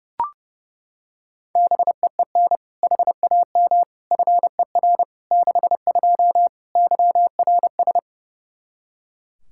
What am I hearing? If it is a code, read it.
Morse code, heard as a keyed tone, about 32 wpm, 700 Hz. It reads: BEEN HAM FER 62 YRS
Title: K1USN SST QSO Practice 30wpm